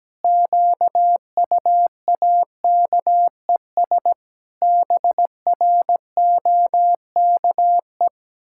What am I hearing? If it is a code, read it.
Morse code, heard as a keyed tone, about 17 wpm, 700 Hz. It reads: QUAKES BROKE
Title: Pangram Challenge v2 - 17wpm